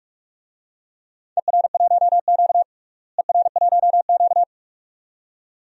Morse code, heard as a keyed tone, about 41 wpm, 700 Hz. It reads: ER1Q ER1Q
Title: DXCC-100 Call Signs - 4 Characters - Repeated 2x - 45wpm